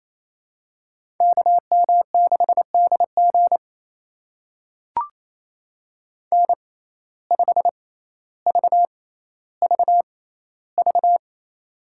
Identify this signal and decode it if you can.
Morse code, heard as a keyed tone, about 28 wpm, spaced as for 14 wpm, 700 Hz. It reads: KM6DG N5VVV
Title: USA Call Signs - 5 Characters - 6x Character Spacing - 28wpm